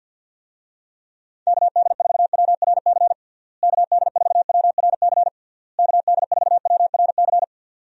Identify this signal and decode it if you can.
Morse code, heard as a keyed tone, about 51 wpm, 700 Hz. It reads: KD4WRC KD4WRC KD4WRC
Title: USA Call Signs - 6 Characters - Repeated 3x - 50wpm